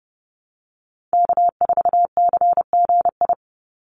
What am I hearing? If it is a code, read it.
Morse code, heard as a keyed tone, about 30 wpm, 700 Hz. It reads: K4CGI
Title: USA Call Signs 30wpm